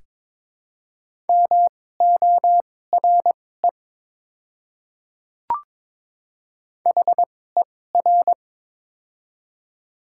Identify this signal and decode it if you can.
Morse code, heard as a keyed tone, about 22 wpm, spaced as for 16 wpm, 700 Hz. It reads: MORE HER
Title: 2x Character Spacing - Top 100 Words - 22wpm